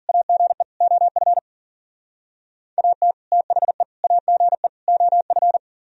Morse code, heard as a keyed tone, about 51 wpm, 700 Hz. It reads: AGE OF AT THE AGE OF